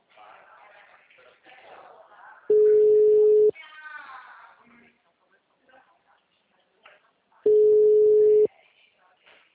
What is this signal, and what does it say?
Ringback tone (Europe, 425 Hz), with speech.